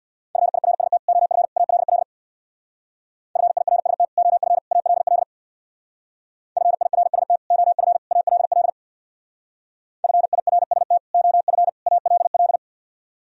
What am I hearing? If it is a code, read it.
Morse code, heard as a keyed tone, about 60 wpm, 700 Hz. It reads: FIRST OF ALL FIRST OF ALL FIRST OF ALL FIRST OF ALL
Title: Speed-Racing - Sets of 3 Words 50wpm